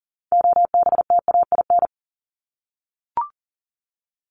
Morse code, heard as a keyed tone, about 40 wpm, 700 Hz. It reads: OBTAIN
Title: Top 1001-1250 Words 40wpm